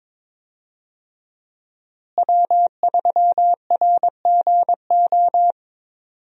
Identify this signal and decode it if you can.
Morse code, heard as a keyed tone, about 22 wpm, 700 Hz. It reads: W3RGO
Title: USA Call Signs - 5 Characters - Repeated 3x - 22wpm